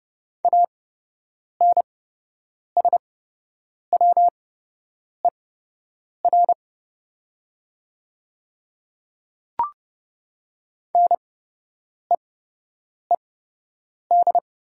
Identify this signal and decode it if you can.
Morse code, heard as a keyed tone, about 30 wpm, spaced as for 8 wpm, 700 Hz. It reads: ANSWER NEED
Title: 8x Character Spacing - Top 500 Words - 30wpm